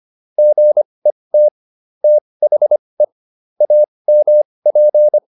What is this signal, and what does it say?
Morse code, heard as a keyed tone, about 25 wpm, spaced as for 20 wpm, 600 Hz. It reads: GET THE AMP